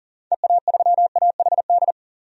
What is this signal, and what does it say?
Morse code, heard as a keyed tone, about 39 wpm, 700 Hz. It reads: EA3AHD
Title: DXCC-100 Call Signs - 6 Characters - 40wpm